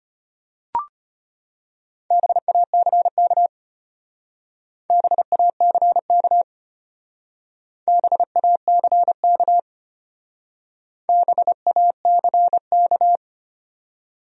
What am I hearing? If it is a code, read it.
Morse code, heard as a keyed tone, about 32 wpm, 700 Hz. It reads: BACK BACK BACK BACK